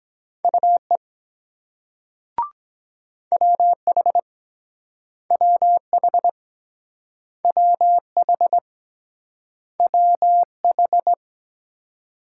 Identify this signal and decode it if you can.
Morse code, heard as a keyed tone, about 23 wpm, 700 Hz. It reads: UE WH WH WH WH